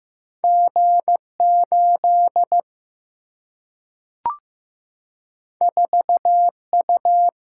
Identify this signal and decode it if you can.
Morse code, heard as a keyed tone, about 15 wpm, 700 Hz. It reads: G8 4U
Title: Single-Letter Single-Number 15wpm